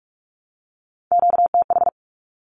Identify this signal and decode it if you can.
Morse code, heard as a keyed tone, about 44 wpm, 700 Hz. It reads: QTH
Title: Q-Codes Repeated 3x 45wpm